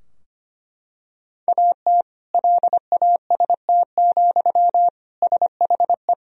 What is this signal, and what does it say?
Morse code, heard as a keyed tone, about 25 wpm, 700 Hz. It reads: AT LAST, SHE
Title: Sentences from Top 1000 Words - Review Part 2 - 25wpm